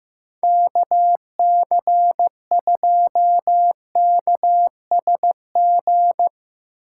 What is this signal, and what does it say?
Morse code, heard as a keyed tone, about 15 wpm, 700 Hz. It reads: KC2KSG
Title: USA Call Signs - 2x Character Spacing - 15wpm